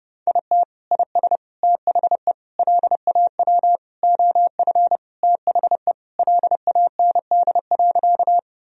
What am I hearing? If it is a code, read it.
Morse code, heard as a keyed tone, about 32 wpm, 700 Hz. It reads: IT IS THE LAW OF THE LAND.
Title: Sentences from Top 1000 Words - Review Part 3 - 30wpm